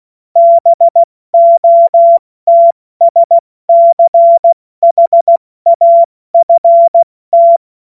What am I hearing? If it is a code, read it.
Morse code, heard as a keyed tone, about 16 wpm, 680 Hz. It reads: BOTSCHAFT
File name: Morsekurs/Witze